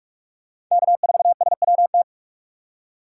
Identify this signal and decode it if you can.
Morse code, heard as a keyed tone, about 42 wpm, 700 Hz. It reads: K4SWT